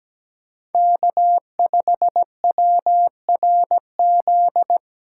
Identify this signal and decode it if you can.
Morse code, heard as a keyed tone, about 17 wpm, 700 Hz. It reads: K5WRZ
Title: USA Call Signs - 5 Characters - 3x Character Spacing - 17wpm